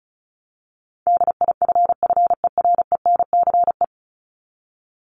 Morse code, heard as a keyed tone, about 34 wpm, 700 Hz. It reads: DIFFERENCE
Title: Top 751-1000 Words 35wpm